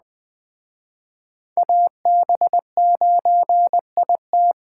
Morse code, heard as a keyed tone, about 24 wpm, 700 Hz. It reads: AB9IT
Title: Speed Racing - USA 2x2 Call Signs 20wpm